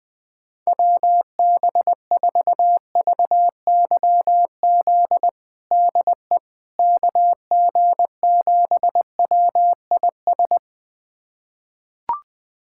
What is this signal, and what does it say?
Morse code, heard as a keyed tone, about 21 wpm, 700 Hz. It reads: WB4VYZ DE KG7WIS
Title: USA Callsign DE Callsign 20wpm